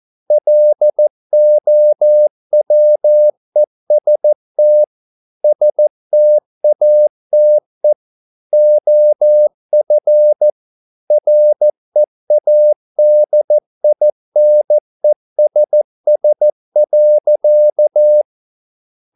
Morse code, heard as a keyed tone, about 14 wpm, 600 Hz. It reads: LOWEST STATE OF READINESS.